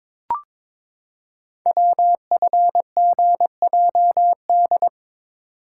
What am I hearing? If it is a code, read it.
Morse code, heard as a keyed tone, about 22 wpm, 700 Hz. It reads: WFGJD
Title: Code Groups - 5 Letter:Numbers - 22wpm